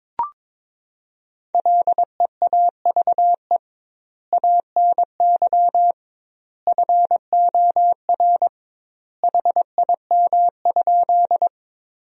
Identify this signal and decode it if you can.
Morse code, heard as a keyed tone, about 22 wpm, 700 Hz. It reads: LEAVE ANY FOR HIM?
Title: 2x Word-Spacing - Sentences from Top 400 Words - 22wpm